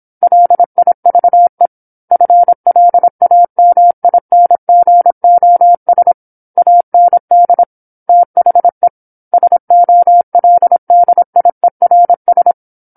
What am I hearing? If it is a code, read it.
Morse code, heard as a keyed tone, about 27 wpm, 700 Hz. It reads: LIVE FLAMINGOS AND THE SOLDIERS
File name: 08-Chapter-08 26 WPM